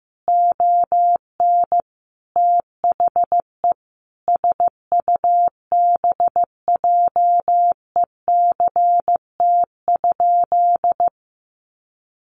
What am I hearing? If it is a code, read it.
Morse code, heard as a keyed tone, about 15 wpm, 700 Hz. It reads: ON THE SUBJECT?
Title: Sentences from Top 800 Words 15wpm